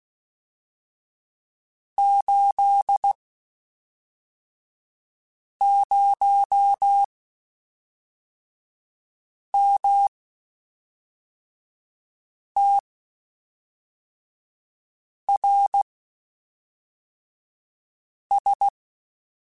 Morse code, heard as a keyed tone, about 16 wpm, spaced as for 3 wpm, 800 Hz. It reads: 80MTRS
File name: (spaced) / QSOs Spaced-mp3s / WD0FBT 1 @5 sp